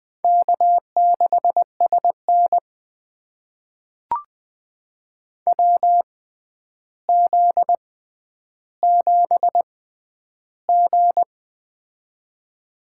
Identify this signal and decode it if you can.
Morse code, heard as a keyed tone, about 20 wpm, spaced as for 10 wpm, 700 Hz. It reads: K6SN WZ7G
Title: USA Call Signs - 4 Characters - 6x Character Spacing - 20wpm